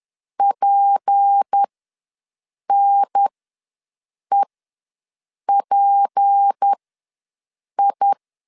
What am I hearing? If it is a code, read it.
Morse code, heard as a keyed tone, about 10 wpm, spaced as for 6 wpm, 800 Hz. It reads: PNEPI
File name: LEZIONI / 6 PKX0 / 6 P ws 10wpm 3751